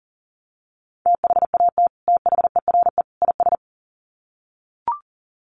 Morse code, heard as a keyed tone, about 40 wpm, 700 Hz. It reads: THAT THERE IS
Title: Sets of 3 Words 40wpm